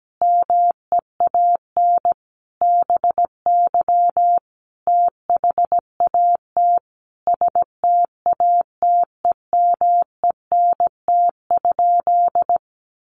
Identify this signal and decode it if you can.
Morse code, heard as a keyed tone, about 17 wpm, 700 Hz. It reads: MEAN BY THAT STATEMENT?
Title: Sentences from Top 1200 Words 17wpm